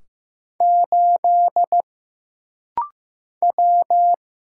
Morse code, heard as a keyed tone, about 15 wpm, 700 Hz. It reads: 8 W
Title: Single Letter-Number Alt - Rapid-Fire 15wpm